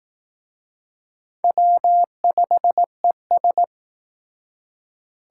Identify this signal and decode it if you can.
Morse code, heard as a keyed tone, about 19 wpm, 700 Hz. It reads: W5ES